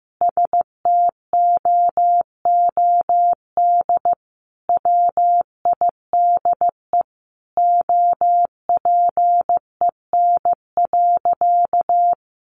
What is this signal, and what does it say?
Morse code, heard as a keyed tone, about 15 wpm, 700 Hz. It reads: STOOD WIDE OPEN.
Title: Sentences from Top 900 Words 15wpm